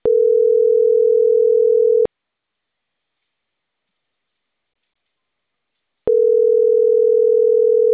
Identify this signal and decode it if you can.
Ringback tone (North America), 440 + 480 Hz.